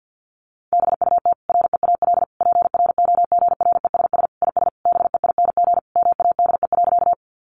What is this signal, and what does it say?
Morse code, heard as a keyed tone, about 50 wpm, 700 Hz. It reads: BUT REAL PROGRESS IS BEING MADE.
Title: Sentences from Top 1400 Words 50wpm